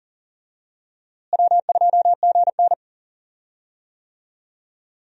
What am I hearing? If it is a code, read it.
Morse code, heard as a keyed tone, about 37 wpm, 700 Hz. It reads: W2GN
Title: USA Call Signs - 4 Characters - Repeated 2x - 40wpm